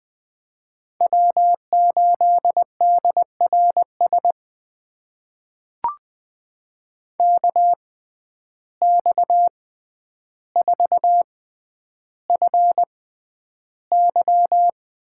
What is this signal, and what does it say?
Morse code, heard as a keyed tone, about 20 wpm, spaced as for 10 wpm, 700 Hz. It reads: W8DRS KX4FY
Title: USA Call Signs - 5 Characters - 6x Character Spacing - 20wpm